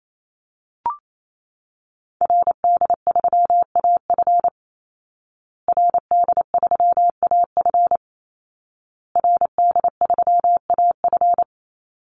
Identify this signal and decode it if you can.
Morse code, heard as a keyed tone, about 28 wpm, 700 Hz. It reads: RD3AF RD3AF RD3AF
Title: Call Signs Repeated 2x - 28wpm